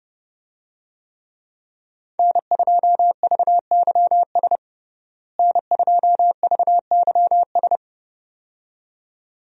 Morse code, heard as a keyed tone, about 30 wpm, 700 Hz. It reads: N2VYS N2VYS
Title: USA Call Signs - 5 Characters - Repeated 2x - 30wpm